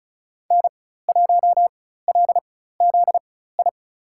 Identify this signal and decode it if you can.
Morse code, heard as a keyed tone, about 36 wpm, spaced as for 16 wpm, 700 Hz. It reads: N1LZI
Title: USA Call Signs - 5 Characters - 4x Character Spacing - 35wpm